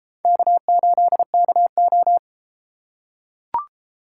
Morse code, heard as a keyed tone, about 33 wpm, 700 Hz. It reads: K8KO